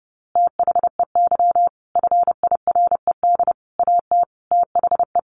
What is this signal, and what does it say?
Morse code, heard as a keyed tone, about 28 wpm, 700 Hz. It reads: THEY FIRED AT THE
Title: A Sound Of Thunder 30wpm